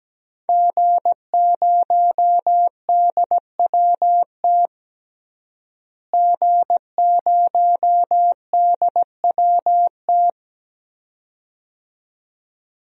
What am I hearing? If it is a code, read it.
Morse code, heard as a keyed tone, about 17 wpm, 700 Hz. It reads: G0DWT G0DWT
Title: DXCC-100 Call Signs - 5 Characters - Repeated 2x - 17wpm